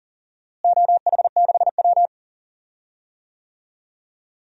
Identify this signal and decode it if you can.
Morse code, heard as a keyed tone, about 40 wpm, 690 Hz. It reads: OH6W
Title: DXCC-100 Call Signs - 4 Characters - Repeated 2x - 40wpm